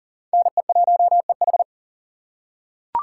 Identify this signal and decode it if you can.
Morse code, heard as a keyed tone, about 40 wpm, 700 Hz. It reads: NE1EH
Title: USA Call Signs - 5 Characters - 2x Character Spacing - 40wpm